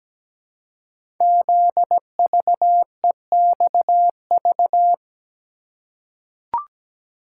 Morse code, heard as a keyed tone, about 17 wpm, 700 Hz. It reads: ZVEXV
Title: Code Groups - 5 Letters - 17wpm